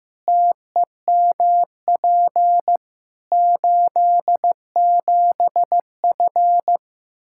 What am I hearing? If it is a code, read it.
Morse code, heard as a keyed tone, about 15 wpm, 700 Hz. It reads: TEMP 87F
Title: Practice QSO - Exchange 2 - Info - 15wpm